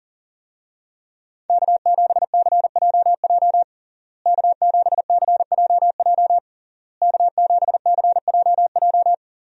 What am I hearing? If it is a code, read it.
Morse code, heard as a keyed tone, about 40 wpm, 700 Hz. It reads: K7CJJ K7CJJ K7CJJ
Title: USA Call Signs - 5 Characters - Repeated 3x - 40wpm